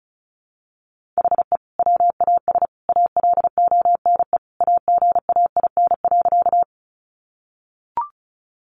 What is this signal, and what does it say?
Morse code, heard as a keyed tone, about 34 wpm, 700 Hz. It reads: HE WAS ALONE AGAIN.
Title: Sentences from Top 1000 Words 35wpm